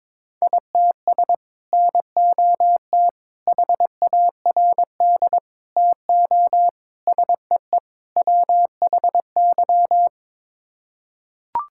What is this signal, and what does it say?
Morse code, heard as a keyed tone, about 22 wpm, 700 Hz. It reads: ITS NOT HARD TO SEE WHY
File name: practice / 22wpm / Sets of 6 Words - 22wpm